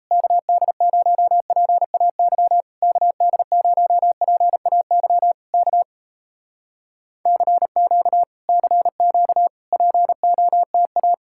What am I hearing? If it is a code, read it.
Morse code, heard as a keyed tone, about 38 wpm, 700 Hz. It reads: KD0PAY KD0PAY K CQ CQ POTA